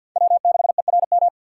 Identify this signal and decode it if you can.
Morse code, heard as a keyed tone, about 47 wpm, 700 Hz. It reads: W6ERM
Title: USA Call Signs - 5 Characters - 2x Character Spacing - 50wpm